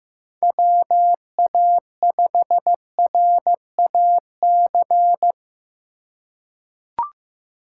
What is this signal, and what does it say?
Morse code, heard as a keyed tone, about 15 wpm, 700 Hz. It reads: WA5RAC
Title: USA Call Signs - 2x Character Spacing - 15wpm